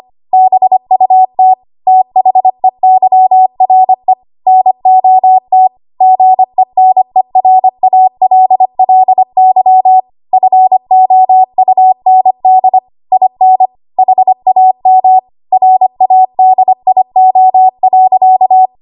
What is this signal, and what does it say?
Morse code, heard as a keyed tone, about 25 wpm, 750 Hz. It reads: BUT THEYRE NOT GENERALLY FOUND IN HAM RADIO.
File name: Morse / 25WPM